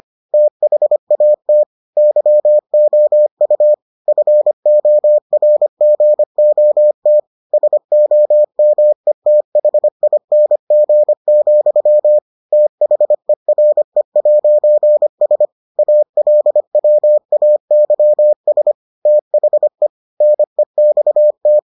Morse code, heard as a keyed tone, about 25 wpm, 600 Hz. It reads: THAT YOU FORGOT SOMETHING, THERE'S ALWAYS THE NEXT